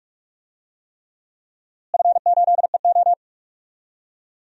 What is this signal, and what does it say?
Morse code, heard as a keyed tone, about 43 wpm, 700 Hz. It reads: F8EO
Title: DXCC-100 Call Signs - 4 Characters - Repeated 3x - 45wpm